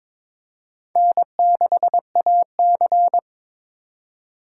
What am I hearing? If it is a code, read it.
Morse code, heard as a keyed tone, about 22 wpm, 700 Hz. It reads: N6AC
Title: USA Call Signs - 2x Character Spacing - 22wpm